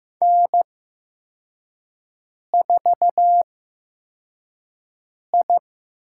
Morse code, heard as a keyed tone, about 15 wpm, spaced as for 4 wpm, 700 Hz. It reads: N4I
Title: USA Call Signs - 5 Characters - 8x Character Spacing - 15wpm